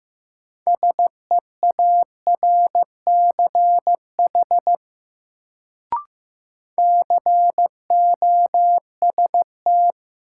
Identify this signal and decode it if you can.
Morse code, heard as a keyed tone, about 15 wpm, 700 Hz. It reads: SEARCH COST
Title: Sentences from Top 800 Words - New Word Drill - 15wpm